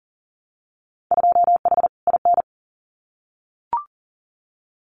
Morse code, heard as a keyed tone, about 41 wpm, 700 Hz. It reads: 2H IN
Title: Winter Field Day Exchanges Encore 40wpm